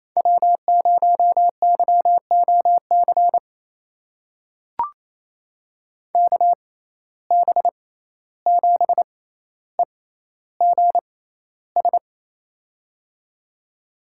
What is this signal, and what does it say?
Morse code, heard as a keyed tone, about 28 wpm, spaced as for 14 wpm, 700 Hz. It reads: W0YOC KB7EGS
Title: USA Call Signs - 6x Character Spacing - 28wpm